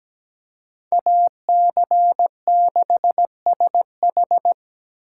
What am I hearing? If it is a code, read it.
Morse code, heard as a keyed tone, about 20 wpm, 700 Hz. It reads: AC6SH
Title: Speed Racing - USA 2x2 Call Signs 17wpm